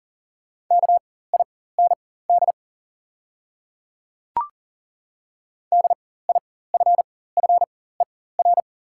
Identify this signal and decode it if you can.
Morse code, heard as a keyed tone, about 41 wpm, spaced as for 19 wpm, 700 Hz. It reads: KIND DIFFER
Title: 4x Character Spacing - Top 300 Words - 40wpm